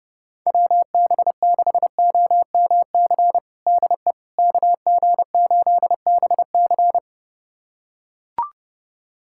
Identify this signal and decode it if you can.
Morse code, heard as a keyed tone, about 30 wpm, 700 Hz. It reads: WB6OMC DE KG8BC
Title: USA Callsign DE Callsign 30wpm